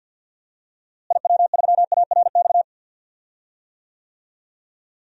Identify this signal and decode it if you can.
Morse code, heard as a keyed tone, about 52 wpm, 700 Hz. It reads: IW3ARX
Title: DXCC-100 Call Signs - 6 Characters - Repeated 3x - 50wpm